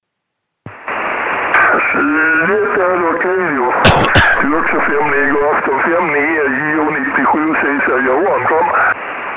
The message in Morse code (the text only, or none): none